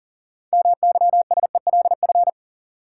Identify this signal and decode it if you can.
Morse code, heard as a keyed tone, about 40 wpm, 700 Hz. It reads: MYSELF
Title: Sentences from Top 1300 Words - New Word Drill - 40wpm